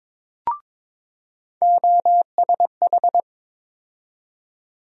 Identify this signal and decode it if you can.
Morse code, heard as a keyed tone, about 22 wpm, 700 Hz. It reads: OSH
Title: USA Call Sign Postfixes - 3 Letters 22wpm